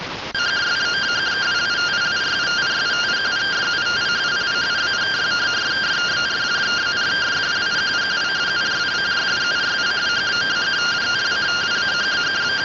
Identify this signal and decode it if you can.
Two-tone teleprinter signal, about 170 Hz shift.